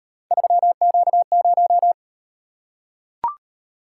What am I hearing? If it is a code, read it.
Morse code, heard as a keyed tone, about 37 wpm, 700 Hz. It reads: 3Q0